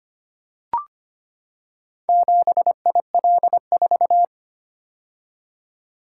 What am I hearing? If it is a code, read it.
Morse code, heard as a keyed tone, about 25 wpm, 700 Hz. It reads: 7IL4
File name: practice / 25wpm / Four Letter-Number Combinations - 25wpm